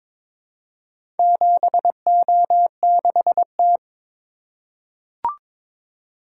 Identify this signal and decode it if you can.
Morse code, heard as a keyed tone, about 22 wpm, 700 Hz. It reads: 7O6T
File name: practice / 22wpm / Four Letter-Number Combinations - 22wpm